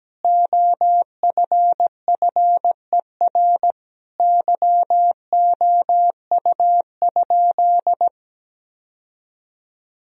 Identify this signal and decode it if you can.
Morse code, heard as a keyed tone, about 17 wpm, 700 Hz. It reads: OFFER YOU?
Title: Sentences from Top 1000 Words - Review Part 4 - 17wpm